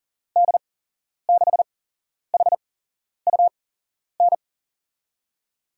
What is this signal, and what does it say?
Morse code, heard as a keyed tone, about 40 wpm, spaced as for 11 wpm, 700 Hz. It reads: D6HUN
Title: USA Call Signs - 6 Characters - 8x Character Spacing - 40wpm